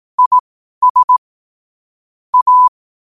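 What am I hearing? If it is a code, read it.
Morse code, heard as a keyed tone, about 18 wpm, spaced as for 12 wpm, 1000 Hz. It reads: IS A
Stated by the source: 12 wpm Morse.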